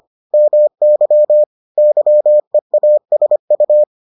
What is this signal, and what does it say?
Morse code, heard as a keyed tone, about 25 wpm, 600 Hz. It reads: MY YEASU